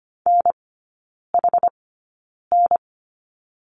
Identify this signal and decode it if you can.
Morse code, heard as a keyed tone, about 25 wpm, spaced as for 9 wpm, 690 Hz. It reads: NHN
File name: Lesson 15e - Number 5 - Three Characters 25wpm